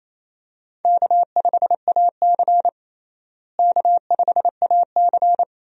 Morse code, heard as a keyed tone, about 28 wpm, 700 Hz. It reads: K5AC K5AC
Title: USA Call Signs - 4 Characters - Repeated 2x - 28wpm